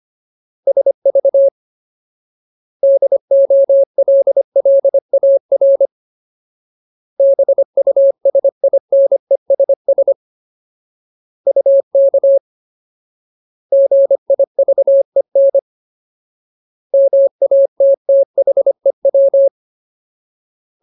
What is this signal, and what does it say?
Morse code, heard as a keyed tone, about 25 wpm, 550 Hz. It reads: SV DOLLAR BUSINESS UK GIVEN MATTHEW
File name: Ultimate Word List / 25WPM Ultimate